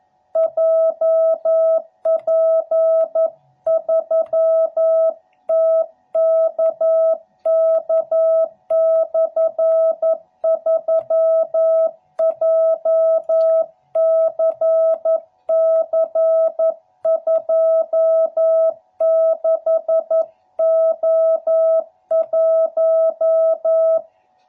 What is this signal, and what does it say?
Morse code, heard as a keyed tone, about 11 wpm, 660 Hz. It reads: JP3TKK/3 JCC26O1
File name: jh8cbh / 2021cw / cw12b